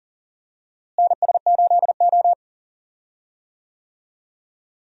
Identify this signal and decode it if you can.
Morse code, heard as a keyed tone, about 40 wpm, 700 Hz. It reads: NS8O
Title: USA Call Signs - 4 Characters - Repeated 2x - 40wpm